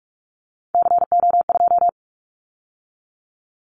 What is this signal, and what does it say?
Morse code, heard as a keyed tone, about 44 wpm, 700 Hz. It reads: CO2
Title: Callsign Prefixes for Top 100 DXCC - 45wpm